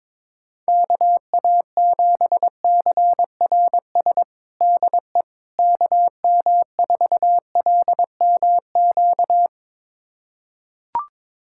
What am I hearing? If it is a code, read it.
Morse code, heard as a keyed tone, about 22 wpm, 700 Hz. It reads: KA7CRS DE KM4LMQ
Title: USA Callsign DE Callsign 22wpm